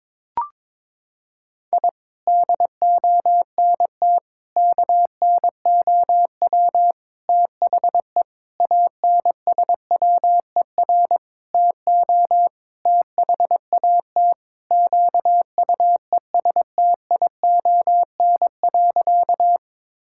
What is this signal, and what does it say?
Morse code, heard as a keyed tone, about 22 wpm, 700 Hz. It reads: I DONT KNOW THE ANSWER TO THAT QUESTION.
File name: Sentences with 7-Word Phrases - 22wpm